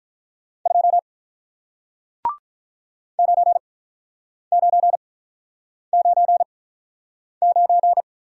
Morse code, heard as a keyed tone, about 48 wpm, spaced as for 9 wpm, 700 Hz. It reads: 2 9999